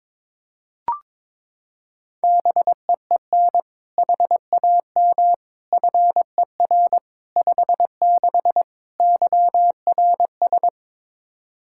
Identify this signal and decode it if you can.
Morse code, heard as a keyed tone, about 22 wpm, 700 Hz. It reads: BEEN HAM FER 56 YRS
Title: Practice QSO - Exchange 3 - More Info - 22wpm